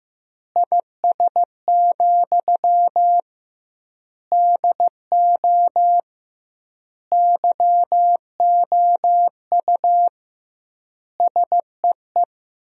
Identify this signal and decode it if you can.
Morse code, heard as a keyed tone, about 16 wpm, 700 Hz. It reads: IS, DO YOU SEE
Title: 2x Word-Spacing - Sentences from Top 100 Words - 15wpm